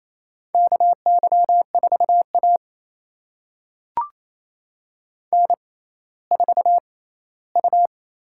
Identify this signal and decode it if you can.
Morse code, heard as a keyed tone, about 28 wpm, spaced as for 16 wpm, 700 Hz. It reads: KY4A N4U
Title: USA Call Signs - 4 Characters - 6x Character Spacing - 28wpm